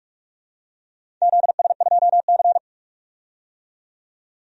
Morse code, heard as a keyed tone, about 45 wpm, 700 Hz. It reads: ZS2C
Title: DXCC-100 Call Signs - 4 Characters - 45wpm